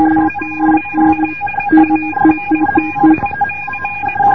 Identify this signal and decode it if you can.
Morse code, heard as a keyed tone, about 20 wpm, 2450 Hz, with noise. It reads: ATLANTIC